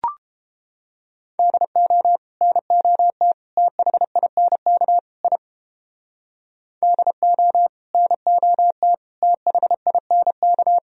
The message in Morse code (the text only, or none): DO NOT THINK I DO NOT THINK